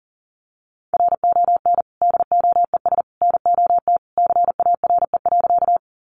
Morse code, heard as a keyed tone, about 40 wpm, 700 Hz. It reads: RON DOES NOT CARE.
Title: Additional Easy Phrases 40wpm